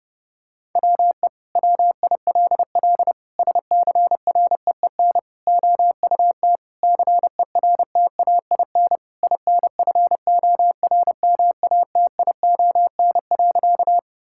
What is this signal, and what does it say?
Morse code, heard as a keyed tone, about 30 wpm, 700 Hz. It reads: WE WILL SCREEN OUT CERTAIN INFORMATION.
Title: Sentences from Top 1000 Words - Review Part 6 - 30wpm